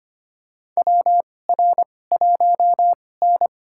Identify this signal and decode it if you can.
Morse code, heard as a keyed tone, about 25 wpm, spaced as for 18 wpm, 700 Hz. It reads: WR1N
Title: USA Call Signs - 4 Characters - 2x Character Spacing - 25wpm